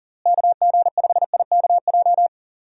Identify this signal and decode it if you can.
Morse code, heard as a keyed tone, about 40 wpm, 700 Hz. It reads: KG5IKJ
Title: USA Call Signs - 6 Characters - Repeated 2x - 40wpm